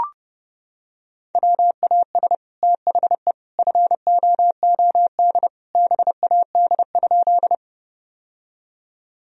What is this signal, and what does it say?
Morse code, heard as a keyed tone, about 30 wpm, 700 Hz. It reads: WAS THE FOOD BAD?
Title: Sentences from Top 1000 Words - Review Part 1 - 30wpm